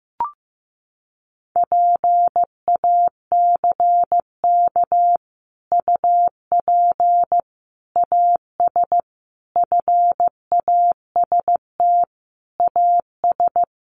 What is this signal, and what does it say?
Morse code, heard as a keyed tone, about 15 wpm, 700 Hz. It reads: PACK UP AS FAST AS
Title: Sentences from Top 1300 Words 15wpm